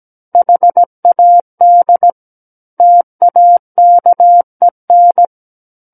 Morse code, heard as a keyed tone, about 17 wpm, 700 Hz. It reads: HAD TAKEN
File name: Book One / 08-Chapter 8 - 17 WPM